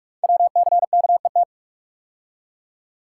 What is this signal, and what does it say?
Morse code, heard as a keyed tone, about 44 wpm, 700 Hz. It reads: WCKET
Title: Code Groups - 5 Letters - 45wpm